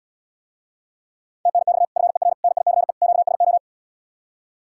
Morse code, heard as a keyed tone, about 78 wpm, 700 Hz. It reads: TNX FER NICE QSO